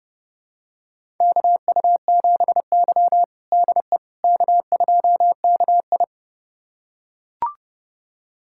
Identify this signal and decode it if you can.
Morse code, heard as a keyed tone, about 30 wpm, 700 Hz. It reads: KU7Y DE K2KI
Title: USA Callsign DE Callsign - 4 Characters - 30wpm